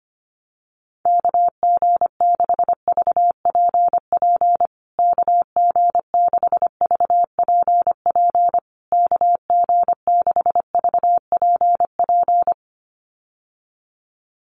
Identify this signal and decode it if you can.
Morse code, heard as a keyed tone, about 25 wpm, 700 Hz. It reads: KG6VPP KG6VPP KG6VPP
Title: USA Call Signs Repeated 3x 25wpm